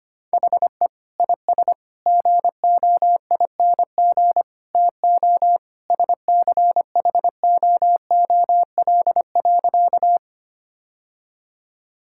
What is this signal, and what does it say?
Morse code, heard as a keyed tone, about 25 wpm, 700 Hz. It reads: HE IS GOING TO SCHOOL.
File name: practice / 25wpm / Sentences with 3-Word Phrases - 25wpm